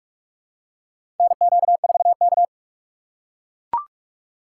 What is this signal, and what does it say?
Morse code, heard as a keyed tone, about 44 wpm, 700 Hz. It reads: NQ4K
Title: USA Call Signs - 4 Characters - 6x Character Spacing - 45wpm